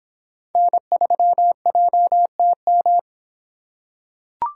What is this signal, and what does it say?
Morse code, heard as a keyed tone, about 26 wpm, 700 Hz. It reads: N3JTM